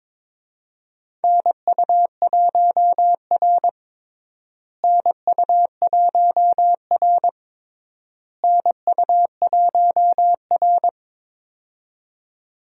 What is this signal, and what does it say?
Morse code, heard as a keyed tone, about 22 wpm, 700 Hz. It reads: NU1R NU1R NU1R
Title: USA Call Signs - 4 Characters - Repeated 3x - 22wpm